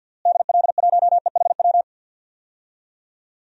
Morse code, heard as a keyed tone, about 52 wpm, 700 Hz. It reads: DL1EHW
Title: DXCC-100 Call Signs - 6 Characters - Repeated 2x - 50wpm